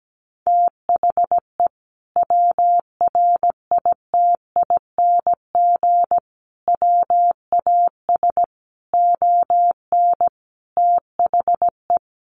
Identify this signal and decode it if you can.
Morse code, heard as a keyed tone, about 17 wpm, 700 Hz. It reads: THE WRITING WAS ON THE
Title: Sentences from Top 1100 Words 17wpm